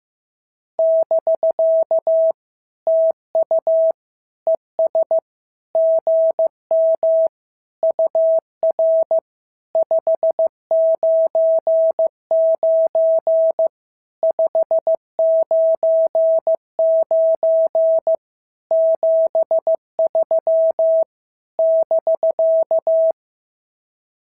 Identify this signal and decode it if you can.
Morse code, heard as a keyed tone, about 14 wpm, 650 Hz. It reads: <BK> TU ES GM UR 599 599 73 <BK>
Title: SOTA QSO Practice 15wpm